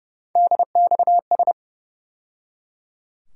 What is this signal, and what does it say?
Morse code, heard as a keyed tone, about 30 wpm, 700 Hz. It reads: DXS
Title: USA Call Sign Postfixes - 3 Letters 30wpm